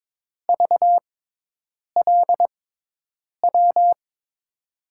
Morse code, heard as a keyed tone, about 22 wpm, spaced as for 8 wpm, 700 Hz. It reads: VLW
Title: USA Call Signs - 6x Character Spacing - 22wpm